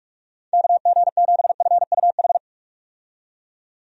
Morse code, heard as a keyed tone, about 45 wpm, 700 Hz. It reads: KG7FUH
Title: USA Call Signs - 6 Characters - 45wpm